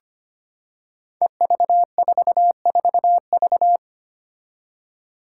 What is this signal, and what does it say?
Morse code, heard as a keyed tone, about 25 wpm, 700 Hz. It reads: EV44V
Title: Five Challenging Letter-Number Combinations 25wpm